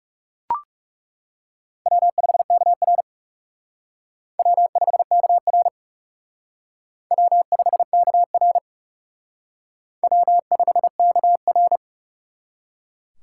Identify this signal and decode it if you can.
Morse code, heard as a keyed tone, about 37 wpm, 700 Hz. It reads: W5KR W5KR W5KR W5KR
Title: Speed Racing - USA Call Signs 30wpm